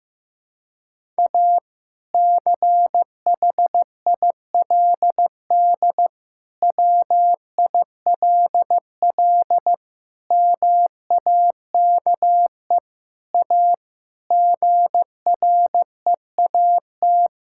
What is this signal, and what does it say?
Morse code, heard as a keyed tone, about 15 wpm, 700 Hz. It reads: A CHILD WILL MAKE A GREAT
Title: Sentences from Top 1000 Words - Review Part 6 - 15wpm